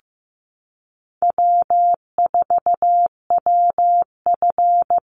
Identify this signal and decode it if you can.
Morse code, heard as a keyed tone, about 15 wpm, 700 Hz. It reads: W4WF
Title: Easy Call Signs Repeated 3x - 15wpm